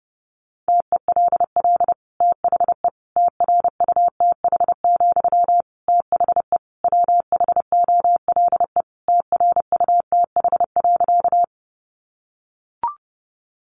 Morse code, heard as a keyed tone, about 30 wpm, 700 Hz. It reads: TELL THE TRUTH, THE WHOLE TRUTH.
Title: Sentences from Top 800 Words 30wpm